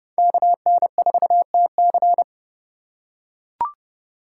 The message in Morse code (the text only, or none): KN4TC